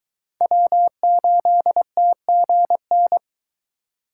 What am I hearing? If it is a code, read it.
Morse code, heard as a keyed tone, about 22 wpm, 700 Hz. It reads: W8TGN